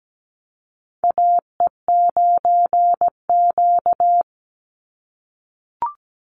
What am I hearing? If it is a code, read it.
Morse code, heard as a keyed tone, about 17 wpm, 700 Hz. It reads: AE9Q
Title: NA SOTA Chaser Callsigns 17wpm